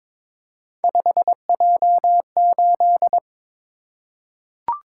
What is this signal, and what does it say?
Morse code, heard as a keyed tone, about 22 wpm, 700 Hz. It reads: 5J8
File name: practice / 22wpm / Three Letter-Number Combinations - 22wpm